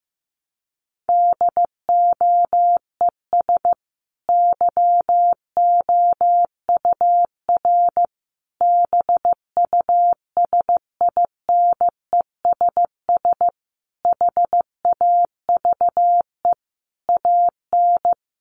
Morse code, heard as a keyed tone, about 15 wpm, 700 Hz. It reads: DOES YOUR BUSINESS HAVE AN
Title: Sentences from Top 1200 Words 15wpm